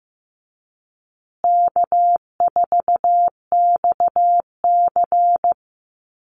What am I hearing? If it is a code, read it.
Morse code, heard as a keyed tone, about 15 wpm, 700 Hz. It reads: K4XC
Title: USA Call Signs Repeated 3x 15wpm